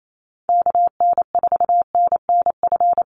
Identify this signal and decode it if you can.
Morse code, heard as a keyed tone, about 28 wpm, 700 Hz. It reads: KN4NNF
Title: USA Call Signs 28wpm